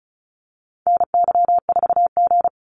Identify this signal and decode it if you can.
Morse code, heard as a keyed tone, about 34 wpm, 700 Hz. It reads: NY4G
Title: NA SOTA Activator Callsigns 35wpm